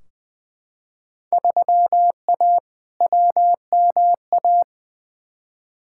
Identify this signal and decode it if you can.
Morse code, heard as a keyed tone, about 20 wpm, 700 Hz. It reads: 3A WMA
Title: Field Day Exchanges Encore - 20wpm